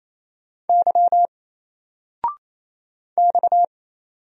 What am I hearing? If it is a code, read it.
Morse code, heard as a keyed tone, about 27 wpm, 700 Hz. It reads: Y X
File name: Single-Letter-Number-28wpm